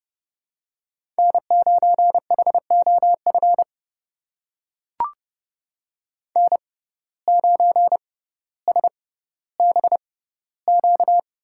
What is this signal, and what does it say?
Morse code, heard as a keyed tone, about 30 wpm, spaced as for 15 wpm, 700 Hz. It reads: N9HOF N9SBQ
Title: USA Call Signs - 5 Characters - 6x Character Spacing - 30wpm